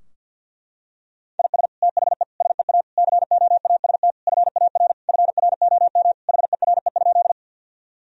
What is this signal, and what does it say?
Morse code, heard as a keyed tone, about 51 wpm, 700 Hz. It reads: IS THE SEA COAST FAR FROM HERE?
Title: Sentences from Top 1000 Words - Review Part 4 - 50wpm